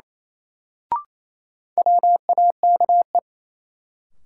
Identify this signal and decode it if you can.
Morse code, heard as a keyed tone, about 27 wpm, 700 Hz. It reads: WAKE
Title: Sets of 4 Words - New Word Drill - 28wpm